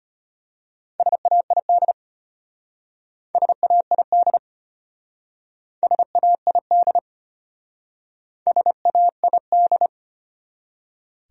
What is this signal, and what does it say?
Morse code, heard as a keyed tone, about 31 wpm, 700 Hz. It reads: SAID SAID SAID SAID
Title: Speed-Racing Top 100 Words 25wpm